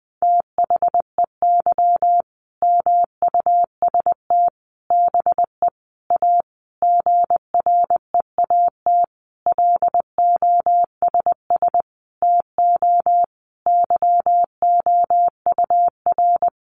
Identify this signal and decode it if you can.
Morse code, heard as a keyed tone, about 20 wpm, 700 Hz. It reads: THEY MUST BE A GREAT LOSS TO YOUR
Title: Sentences from Top 1400 Words 20wpm